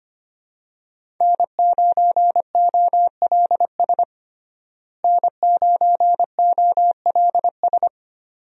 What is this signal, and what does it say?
Morse code, heard as a keyed tone, about 25 wpm, 700 Hz. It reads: N9OLS N9OLS
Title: USA Call Signs - 5 Characters - Repeated 2x - 25wpm